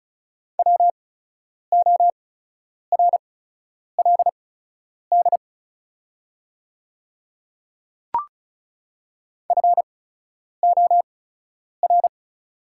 Morse code, heard as a keyed tone, about 36 wpm, spaced as for 10 wpm, 700 Hz. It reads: WORLD FOR
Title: 8x Character Spacing - Top 300 Words - 35wpm